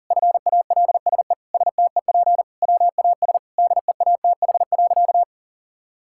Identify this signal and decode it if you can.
Morse code, heard as a keyed tone, about 40 wpm, 700 Hz. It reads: FALSE STEP WAS DEATH.
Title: Sentences from Top 1600 Words 40wpm